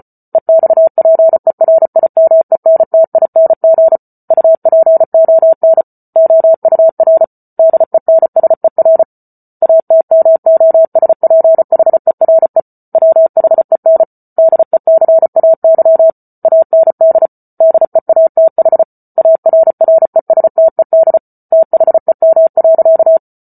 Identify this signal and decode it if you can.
Morse code, heard as a keyed tone, about 34 wpm, 650 Hz. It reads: EXPERIMENTING UPON OUR DENSER ATMOSPHERE WHEN DECAY AND DEATH ARRESTED THEM.